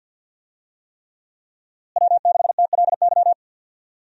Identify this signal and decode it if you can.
Morse code, heard as a keyed tone, about 50 wpm, 700 Hz. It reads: W6TLY
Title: USA Call Signs - 5 Characters - Repeated 3x - 50wpm